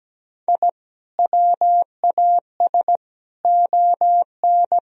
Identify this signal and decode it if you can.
Morse code, heard as a keyed tone, about 17 wpm, 700 Hz. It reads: I WAS ON
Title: Sentences from Top 1000 Words - Review Part 2 - 17wpm